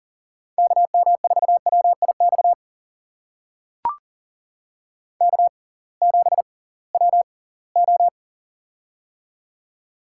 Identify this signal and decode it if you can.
Morse code, heard as a keyed tone, about 40 wpm, spaced as for 23 wpm, 700 Hz. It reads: KM4WIX K7WO
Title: USA Call Signs - 6x Character Spacing - 40wpm